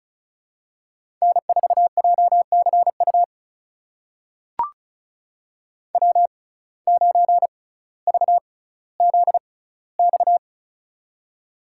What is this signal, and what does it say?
Morse code, heard as a keyed tone, about 36 wpm, spaced as for 18 wpm, 700 Hz. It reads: N4JCU W9VZX
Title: USA Call Signs - 5 Characters - 6x Character Spacing - 35wpm